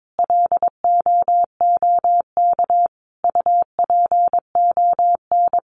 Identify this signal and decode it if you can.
Morse code, heard as a keyed tone, about 22 wpm, 700 Hz. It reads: LOOK UPON
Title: Sentences from Top 1500 Words 22wpm